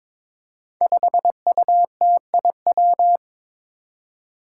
Morse code, heard as a keyed tone, about 22 wpm, 700 Hz. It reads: 5UTIW
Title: Five Challenging Letter-Number Combinations 22wpm